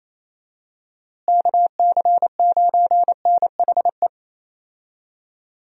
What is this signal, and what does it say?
Morse code, heard as a keyed tone, about 28 wpm, 700 Hz. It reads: KC9NHE
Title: USA Call Signs - 6 Characters - Repeated 3x - 28wpm